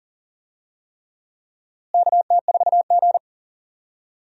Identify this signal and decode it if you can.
Morse code, heard as a keyed tone, about 40 wpm, 700 Hz. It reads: KT4G
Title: USA Call Signs - 4 Characters - Repeated 3x - 40wpm